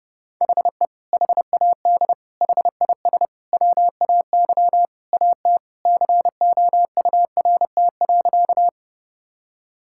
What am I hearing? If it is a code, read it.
Morse code, heard as a keyed tone, about 30 wpm, 700 Hz. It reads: HE HAD HIS WAY AT COURT.
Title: Sentences from Top 1600 Words 30wpm